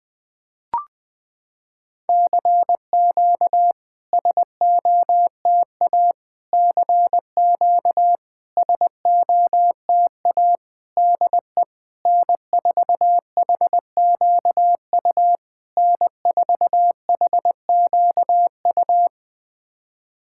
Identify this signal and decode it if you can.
Morse code, heard as a keyed tone, about 20 wpm, 690 Hz. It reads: CQ SOTA CQ SOTA DE N4HQU N4HQU
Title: SOTA QSO Practice 20wpm